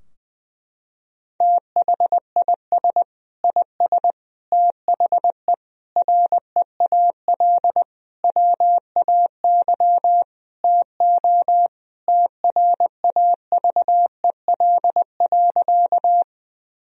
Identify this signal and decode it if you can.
Morse code, heard as a keyed tone, about 20 wpm, 700 Hz. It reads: THIS IS THE REAL WAY TO TRAVEL.
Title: Sentences from Top 1000 Words - Review Part 2 - 20wpm